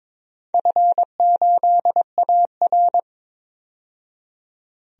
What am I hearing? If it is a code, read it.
Morse code, heard as a keyed tone, about 21 wpm, 700 Hz. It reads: F8AR
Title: DXCC-100 Call Signs - 4 Characters - 22wpm